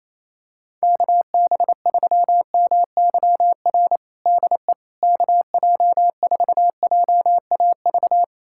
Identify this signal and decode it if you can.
Morse code, heard as a keyed tone, about 28 wpm, 700 Hz. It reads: KB3MYR DE KJ4JAV
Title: USA Callsign DE Callsign - 6 Characters - 28wpm